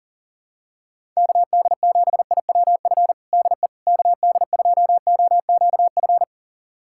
Morse code, heard as a keyed tone, about 37 wpm, 700 Hz. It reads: KD7IWF DE KD2OQF
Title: USA Callsign DE Callsign - 6 Characters - 40wpm